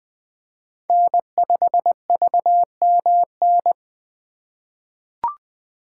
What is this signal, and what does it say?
Morse code, heard as a keyed tone, about 20 wpm, 700 Hz. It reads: N5VMN
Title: 2023 POTA Hunter Call Signs - 20wpm